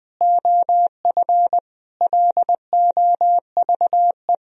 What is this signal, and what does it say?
Morse code, heard as a keyed tone, about 20 wpm, 700 Hz. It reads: OF LOVE
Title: Sentences from Top 1000 Words - Review Part 4 - 20wpm